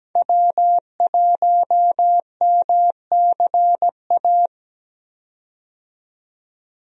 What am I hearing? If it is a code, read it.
Morse code, heard as a keyed tone, about 17 wpm, 700 Hz. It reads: W1MCA
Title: USA Call Signs - 5 Characters - 17wpm